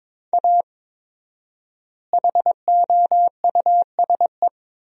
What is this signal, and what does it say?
Morse code, heard as a keyed tone, about 21 wpm, 700 Hz. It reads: A HOUSE
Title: 4x Word-Spacing - Sentences from Top 400 Words - 22wpm